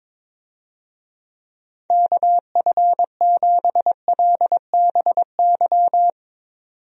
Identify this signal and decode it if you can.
Morse code, heard as a keyed tone, about 22 wpm, 700 Hz. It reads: KF7LBY
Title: USA Call Signs - 6 Characters - 22wpm